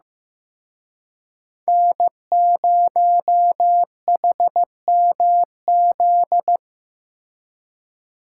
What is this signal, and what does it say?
Morse code, heard as a keyed tone, about 18 wpm, 700 Hz. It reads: N0HMZ
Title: Speed Racing - USA 1x3 Call Signs 15wpm